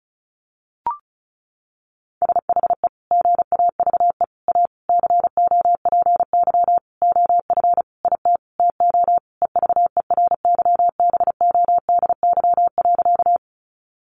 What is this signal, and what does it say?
Morse code, heard as a keyed tone, about 34 wpm, 700 Hz. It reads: SHE GAVE A COPY OF IT TO EVERYBODY.
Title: Sentences from Top 1400 Words 35wpm